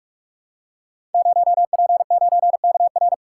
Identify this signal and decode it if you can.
Morse code, heard as a keyed tone, about 44 wpm, 700 Hz. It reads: 0P9KR
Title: Code Groups - 5 Letter:Numbers - 45wpm